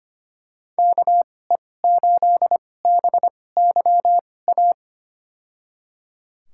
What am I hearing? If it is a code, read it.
Morse code, heard as a keyed tone, about 25 wpm, spaced as for 18 wpm, 700 Hz. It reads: KE8BYA
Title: USA Call Signs - 6 Characters - 2x Character Spacing - 25wpm